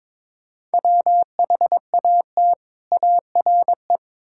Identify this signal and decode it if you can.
Morse code, heard as a keyed tone, about 22 wpm, 700 Hz. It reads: WHAT ARE
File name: Build A Phrase - 5-Word Phrases - 22wpm